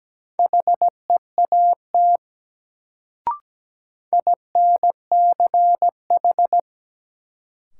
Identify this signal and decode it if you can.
Morse code, heard as a keyed tone, about 17 wpm, 700 Hz. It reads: HEAT INCH
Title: Sentences from Top 1000 Words - Review Part 3 - New Word Drill - 17wpm